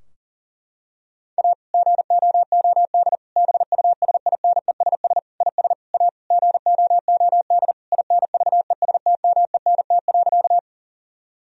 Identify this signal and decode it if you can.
Morse code, heard as a keyed tone, about 40 wpm, 700 Hz. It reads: A GOOD BUSINESS IS A GOOD INVESTMENT.
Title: Sentences from Top 1000 Words - Review Part 4 - 40wpm